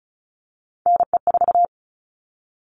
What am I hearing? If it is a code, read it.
Morse code, heard as a keyed tone, about 34 wpm, 700 Hz. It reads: NE4
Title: All Cut Numbers - 3-Digit 35wpm